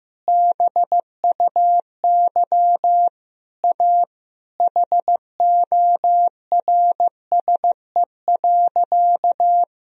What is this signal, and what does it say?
Morse code, heard as a keyed tone, about 16 wpm, 700 Hz. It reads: BUY A HORSE.
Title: Sentences from Top 1000 Words - Review Part 4 - 15wpm